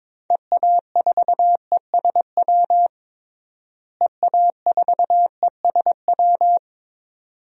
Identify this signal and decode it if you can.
Morse code, heard as a keyed tone, about 22 wpm, 700 Hz. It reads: EA4ESW EA4ESW
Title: DXCC-100 Call Signs - 6 Characters - Repeated 2x - 22wpm